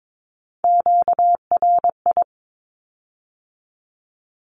Morse code, heard as a keyed tone, about 22 wpm, 700 Hz. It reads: QRI
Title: Q-Codes with Spoken Meaning - 22wpm